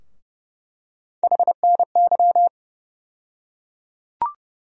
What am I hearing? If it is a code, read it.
Morse code, heard as a keyed tone, about 30 wpm, 700 Hz. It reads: HNY
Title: USA Call Sign Postfixes - 3 Letters 30wpm